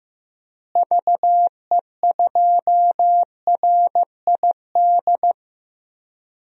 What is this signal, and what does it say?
Morse code, heard as a keyed tone, about 15 wpm, 700 Hz. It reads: VE2RID
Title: DXCC-100 Call Signs - 6 Characters - Repeated 3x - 15wpm